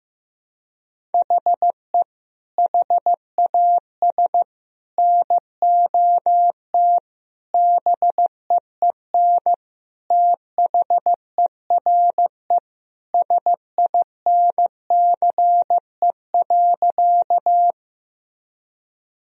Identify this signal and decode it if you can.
Morse code, heard as a keyed tone, about 16 wpm, 700 Hz. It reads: HE HAS NOT BEEN THERE SINCE.
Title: Sentences from Top 1000 Words - Review Part 1 - 15wpm